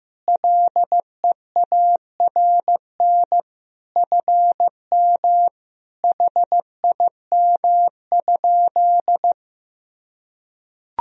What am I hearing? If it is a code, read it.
Morse code, heard as a keyed tone, about 15 wpm, 700 Hz. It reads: LEARN FM HIM?
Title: Sentences from Top 300 Words and Abbreviations 15wpm